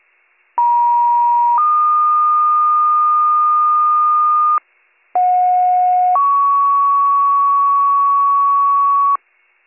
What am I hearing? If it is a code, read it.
Two-tone page, heard as a one-second tone, then a three-second tone, about 950 then 1220 Hz, about 730 then 1120 Hz. It is heard from 0.6 to 9.2 s.